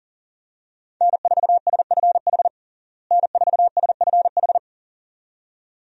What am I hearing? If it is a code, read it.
Morse code, heard as a keyed tone, about 40 wpm, 700 Hz. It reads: N4SFH N4SFH
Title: USA Call Signs - 5 Characters - Repeated 2x - 40wpm